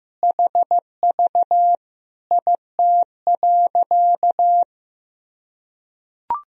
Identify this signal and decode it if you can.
Morse code, heard as a keyed tone, about 15 wpm, 700 Hz. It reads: HV IT.
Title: Sentences from Top 100 Words and Abbreviations 15wpm